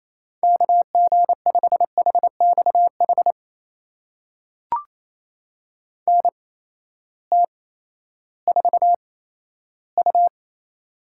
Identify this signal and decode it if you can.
Morse code, heard as a keyed tone, about 28 wpm, spaced as for 14 wpm, 700 Hz. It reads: KG5HXH NT4U
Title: USA Call Signs - 8x Character Spacing - 28wpm